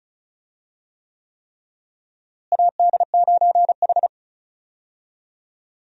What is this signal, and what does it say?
Morse code, heard as a keyed tone, about 34 wpm, 700 Hz. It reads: AD9H
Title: USA Call Signs - 4 Characters - Repeated 3x - 35wpm